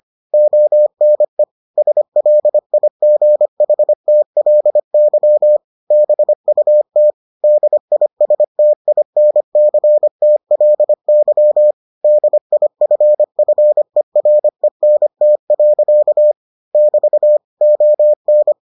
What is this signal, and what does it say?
Morse code, heard as a keyed tone, about 25 wpm, 600 Hz. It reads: ONE SLIGHTLY BUT DISTINCTLY DIFFERENT. <BT> ON